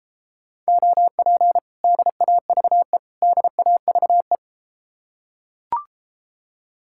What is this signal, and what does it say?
Morse code, heard as a keyed tone, about 33 wpm, 700 Hz. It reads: OP DAVE DAVE